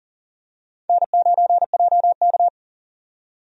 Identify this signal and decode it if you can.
Morse code, heard as a keyed tone, about 40 wpm, 700 Hz. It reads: N9JK
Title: USA Call Signs - 4 Characters - 40wpm